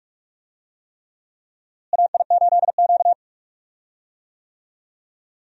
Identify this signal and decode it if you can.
Morse code, heard as a keyed tone, about 46 wpm, 700 Hz. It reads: AI8Q
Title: USA Call Signs - 4 Characters - Repeated 3x - 45wpm